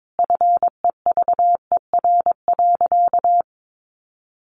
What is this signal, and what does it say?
Morse code, heard as a keyed tone, about 22 wpm, 700 Hz. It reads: FEVER.
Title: Easy Phrases 22wpm